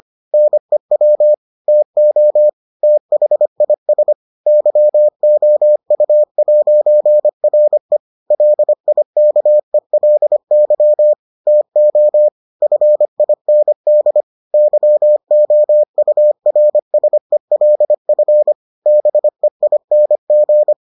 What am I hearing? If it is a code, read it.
Morse code, heard as a keyed tone, about 25 wpm, 600 Hz. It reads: NEW TO THIS YOU'RE LIKELY TO FIND YOURSELF BEING